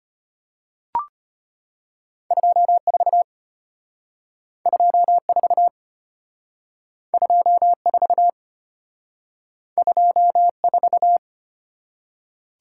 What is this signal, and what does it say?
Morse code, heard as a keyed tone, about 33 wpm, 700 Hz. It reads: 24 24 24 24